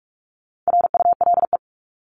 Morse code, heard as a keyed tone, about 44 wpm, 700 Hz. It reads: RULE
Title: ES:S Suffix - Top 100 Words with Base Words 45wpm